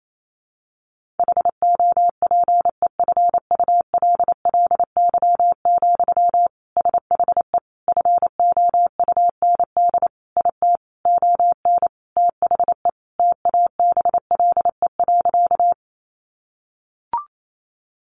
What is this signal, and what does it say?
Morse code, heard as a keyed tone, about 28 wpm, 700 Hz. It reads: HOPEFULLY, SHE FOUND IT ON THE TABLE.
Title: Sentences from Top 1500 Words 28wpm